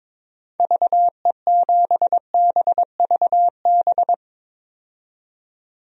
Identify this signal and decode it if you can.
Morse code, heard as a keyed tone, about 22 wpm, 700 Hz. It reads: VE7BVB
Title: DXCC-100 Call Signs - 6 Characters - 22wpm